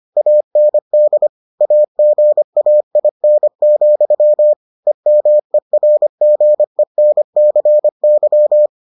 Morse code, heard as a keyed tone, about 25 wpm, 600 Hz. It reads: AND AGAIN, EMERGENCY